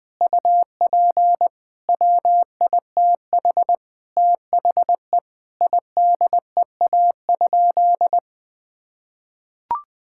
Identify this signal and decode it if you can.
Morse code, heard as a keyed tone, about 20 wpm, 700 Hz. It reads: UP WITH THE IDEA?
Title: Sentences with 8-Word Phrases - Spoken Beginning - 20wpm